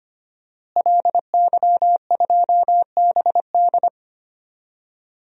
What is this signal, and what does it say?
Morse code, heard as a keyed tone, about 25 wpm, 700 Hz. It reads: LY2BD
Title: DXCC-100 Call Signs - 5 Characters - Repeated 3x - 25wpm